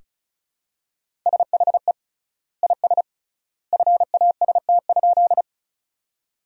Morse code, heard as a keyed tone, about 36 wpm, 700 Hz. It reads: SHE IS FAST?
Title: 3x Word-Spacing - Sentences from Top 400 Words - 35wpm